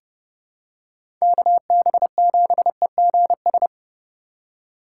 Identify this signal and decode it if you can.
Morse code, heard as a keyed tone, about 30 wpm, 700 Hz. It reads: KB7EGS
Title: USA Call Signs - 2x Character Spacing - 30wpm